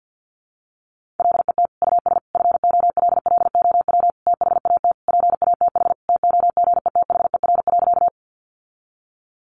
Morse code, heard as a keyed tone, about 50 wpm, 700 Hz. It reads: LET US FOLLOW THAT PATH TOGETHER.
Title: Sentences from Top 1400 Words 50wpm